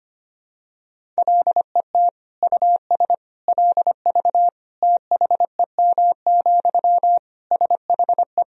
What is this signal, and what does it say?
Morse code, heard as a keyed tone, about 25 wpm, 700 Hz. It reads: LET US LV THEM, SHE
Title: Sentences from Top 400 Words and Abbreviations 25wpm